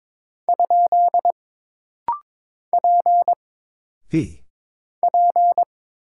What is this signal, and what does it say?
Morse code, heard as a keyed tone, about 22 wpm, 700 Hz. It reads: ? P P
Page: This rapid-fire timing is geared to push your ICR (Instant Character Recognition) skills.